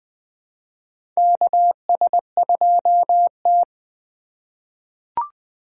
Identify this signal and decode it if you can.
Morse code, heard as a keyed tone, about 20 wpm, 700 Hz. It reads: KS2T
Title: USA Call Signs - 3x Character Spacing - 20wpm